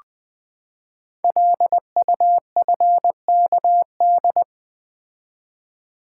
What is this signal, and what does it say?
Morse code, heard as a keyed tone, about 21 wpm, 700 Hz. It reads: LUFKD
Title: Code Groups - 5 Letters - 20wpm